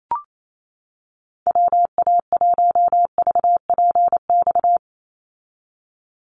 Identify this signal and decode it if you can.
Morse code, heard as a keyed tone, about 28 wpm, 700 Hz. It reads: WA1VPX
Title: USA Call Signs 28wpm